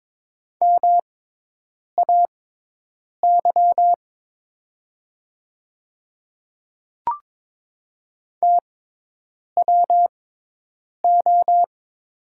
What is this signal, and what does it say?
Morse code, heard as a keyed tone, about 22 wpm, spaced as for 8 wpm, 700 Hz. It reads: MAY TWO
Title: 6x Character Spacing - Top 100 Words - 22wpm